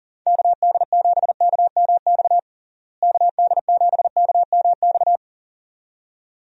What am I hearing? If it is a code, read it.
Morse code, heard as a keyed tone, about 37 wpm, 700 Hz. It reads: KD7KMX KD7KMX
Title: USA Call Signs - 6 Characters - Repeated 2x - 40wpm